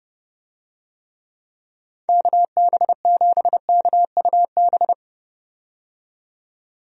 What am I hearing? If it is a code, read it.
Morse code, heard as a keyed tone, about 32 wpm, 700 Hz. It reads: KB7KUB
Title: USA Call Signs - 6 Characters - 30wpm